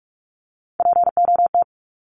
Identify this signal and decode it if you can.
Morse code, heard as a keyed tone, about 44 wpm, 700 Hz. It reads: POT
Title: Top 751-1000 Words 45wpm